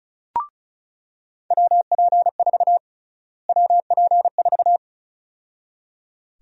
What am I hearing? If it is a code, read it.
Morse code, heard as a keyed tone, about 36 wpm, 700 Hz. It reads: WP4 WP4
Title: USA Call Sign Prefixes - Repeated 2x - 35wpm